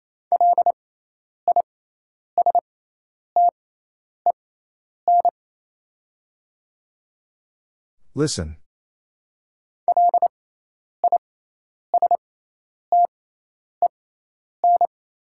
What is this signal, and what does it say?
Morse code, heard as a keyed tone, about 28 wpm, spaced as for 10 wpm, 700 Hz. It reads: LISTEN LISTEN
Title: 6x Character Spacing - Top 500 Words - 28wpm